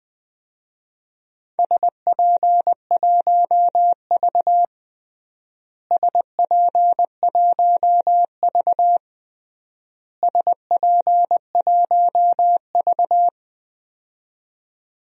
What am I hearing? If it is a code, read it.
Morse code, heard as a keyed tone, about 20 wpm, 700 Hz. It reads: SP1V SP1V SP1V
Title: DXCC-100 Call Signs - 4 Characters - Repeated 3x - 20wpm